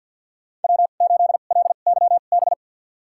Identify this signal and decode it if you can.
Morse code, heard as a keyed tone, about 52 wpm, spaced as for 37 wpm, 700 Hz. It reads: W8LYB
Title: USA Call Signs - 5 Characters - 2x Character Spacing - 50wpm